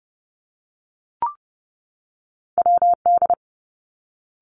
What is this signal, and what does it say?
Morse code, heard as a keyed tone, about 30 wpm, 700 Hz. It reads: W D
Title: Two Challenging Letter-Number Combinations 30wpm